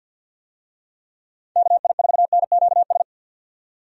Morse code, heard as a keyed tone, about 50 wpm, 700 Hz. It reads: KI4NQS
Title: USA Call Signs - 6 Characters - 50wpm